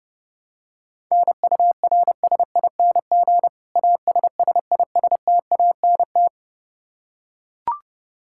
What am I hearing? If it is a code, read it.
Morse code, heard as a keyed tone, about 32 wpm, 700 Hz. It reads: NURSING ASSISTANT
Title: Practice QSO - Exchange 3 - More Info - 30wpm